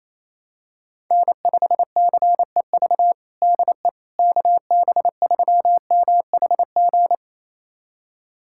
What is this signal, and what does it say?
Morse code, heard as a keyed tone, about 28 wpm, 700 Hz. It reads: N5CEV DE KB3MHG
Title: USA Callsign DE Callsign 28wpm